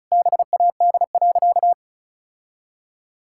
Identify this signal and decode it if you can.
Morse code, heard as a keyed tone, about 34 wpm, 700 Hz. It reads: BAD.
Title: Sentences from Top 1000 Words - Review Part 5 - 35wpm